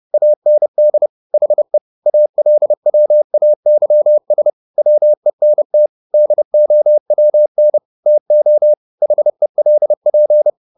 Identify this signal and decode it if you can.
Morse code, heard as a keyed tone, about 30 wpm, 600 Hz. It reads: AND HE ALWAYS WENT DOWN TO HELP
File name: the old man and the sea / the old man and the sea 1 30wpm